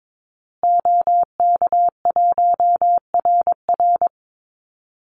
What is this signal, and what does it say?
Morse code, heard as a keyed tone, about 22 wpm, 690 Hz. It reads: OK1RR
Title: Call Signs Repeated 2x - 22wpm